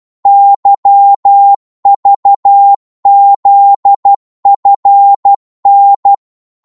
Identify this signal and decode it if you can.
Morse code, heard as a keyed tone, about 12 wpm, 800 Hz. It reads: YVZFN